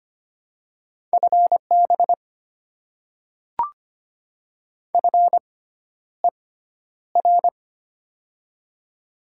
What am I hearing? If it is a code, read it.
Morse code, heard as a keyed tone, about 25 wpm, spaced as for 11 wpm, 700 Hz. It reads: FB FER
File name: Top 90 QSO Elements - 6x Character Spacing - 25wpm